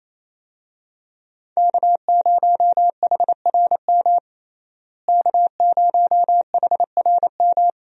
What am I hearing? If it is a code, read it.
Morse code, heard as a keyed tone, about 28 wpm, 700 Hz. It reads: K0HRM K0HRM
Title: USA Call Signs - 5 Characters - Repeated 2x - 28wpm